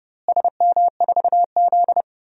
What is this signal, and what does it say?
Morse code, heard as a keyed tone, about 30 wpm, 700 Hz. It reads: SM4Z
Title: DXCC-100 Call Signs - 4 Characters - Repeated 2x - 30wpm